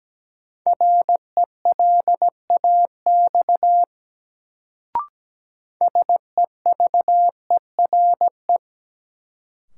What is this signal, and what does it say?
Morse code, heard as a keyed tone, about 17 wpm, 700 Hz. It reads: RELAX SEVERE
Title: Sentences from Top 1400 Words - New Word Drill Alt - 17wpm